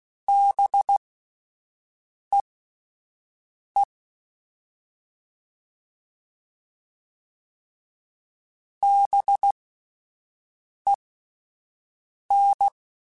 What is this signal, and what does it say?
Morse code, heard as a keyed tone, about 16 wpm, spaced as for 5 wpm, 800 Hz. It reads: BEE BEN
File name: Spaced Alphas / Spaced Alphas mp3s / Bs sp @10